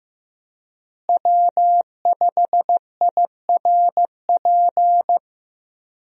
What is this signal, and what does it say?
Morse code, heard as a keyed tone, about 15 wpm, 700 Hz. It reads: W5IRP
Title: USA Call Signs - 5 Characters - Repeated 3x - 15wpm